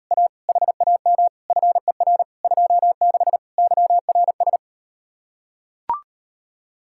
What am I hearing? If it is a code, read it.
Morse code, heard as a keyed tone, about 38 wpm, 700 Hz. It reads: A HAM FER 26 YRS